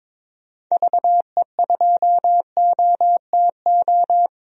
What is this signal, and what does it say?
Morse code, heard as a keyed tone, about 22 wpm, 700 Hz. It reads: VE2OTO
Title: DXCC-100 Call Signs - 6 Characters - 22wpm